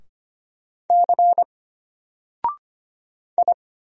Morse code, heard as a keyed tone, about 25 wpm, 700 Hz. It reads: C I
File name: Single-Letter-Number-25wpm